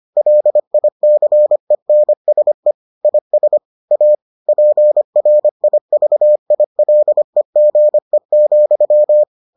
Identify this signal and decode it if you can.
Morse code, heard as a keyed tone, about 25 wpm, 600 Hz. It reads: LICENSE IS A PRIVILEGE,